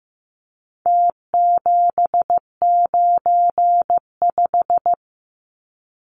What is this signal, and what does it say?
Morse code, heard as a keyed tone, about 15 wpm, 700 Hz. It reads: T795
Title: Numbers - Contest Style - 4-Digit 15wpm